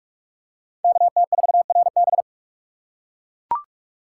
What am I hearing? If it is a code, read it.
Morse code, heard as a keyed tone, about 39 wpm, 700 Hz. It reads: KT4RB